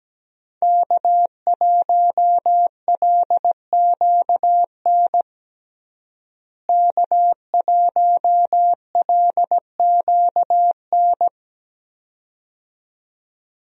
Morse code, heard as a keyed tone, about 17 wpm, 700 Hz. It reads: K1LQN K1LQN
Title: USA Call Signs - 5 Characters - Repeated 2x - 17wpm